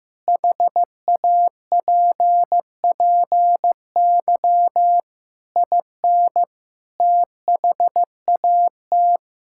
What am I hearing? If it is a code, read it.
Morse code, heard as a keyed tone, about 15 wpm, 700 Hz. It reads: HAPPY IN THAT
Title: Sentences from Top 1600 Words 15wpm